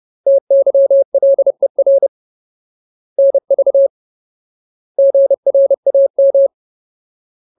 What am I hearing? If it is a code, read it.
Morse code, heard as a keyed tone, about 30 wpm, 550 Hz. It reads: TYLER NV GRAM
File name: Ultimate Word List / 30WPM Ultimate